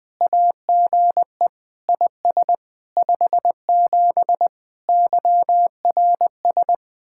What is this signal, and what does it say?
Morse code, heard as a keyed tone, about 20 wpm, 700 Hz. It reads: AGE IS 57 YRS
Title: Speed-Racing Common QSO Phrases - 17wpm